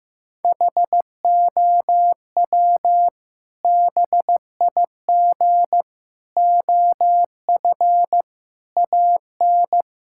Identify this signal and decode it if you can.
Morse code, heard as a keyed tone, about 15 wpm, 700 Hz. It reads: HOW BIG OF AN
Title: Sentences from Top 1000 Words - Review Part 6 - 15wpm